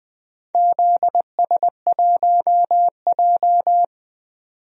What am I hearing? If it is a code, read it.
Morse code, heard as a keyed tone, about 20 wpm, 700 Hz. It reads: ZS1J
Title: DXCC-100 Call Signs - 4 Characters - 20wpm